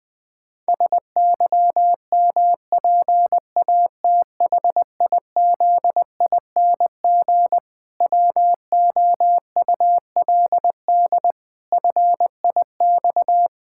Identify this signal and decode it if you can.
Morse code, heard as a keyed tone, about 20 wpm, 700 Hz. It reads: SYMPATHIZING WOULD FIX
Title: Pangram Challenge v2 - 20wpm